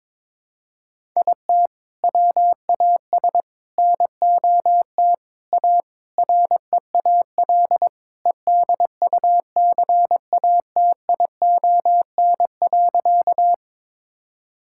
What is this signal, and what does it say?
Morse code, heard as a keyed tone, about 22 wpm, 700 Hz. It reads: IT WAS NOT A REAL EDUCATION.
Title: Sentences from Top 1000 Words - Review Part 6 - 22wpm